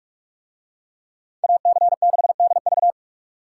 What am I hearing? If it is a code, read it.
Morse code, heard as a keyed tone, about 44 wpm, 700 Hz. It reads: AC6DV